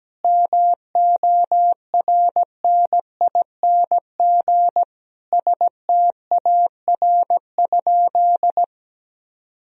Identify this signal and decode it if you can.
Morse code, heard as a keyed tone, about 17 wpm, 700 Hz. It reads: MORNING STAR?
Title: Sentences from Top 1000 Words - Review Part 2 - 17wpm